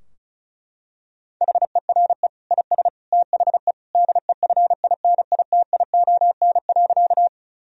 Morse code, heard as a keyed tone, about 36 wpm, 700 Hz. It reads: HERE IS THE DEFINITION.
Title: Sentences from Top 1000 Words - Review Part 4 - 35wpm